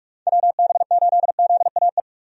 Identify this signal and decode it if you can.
Morse code, heard as a keyed tone, about 46 wpm, 700 Hz. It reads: WB8ZAE
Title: USA Call Signs - 6 Characters - 45wpm